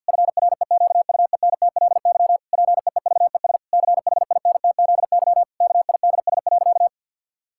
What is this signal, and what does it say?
Morse code, heard as a keyed tone, about 50 wpm, 700 Hz. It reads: FREQUENTLY PEEVES CHINTZY KIDS.
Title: Pangram Challenge v2 - 50wpm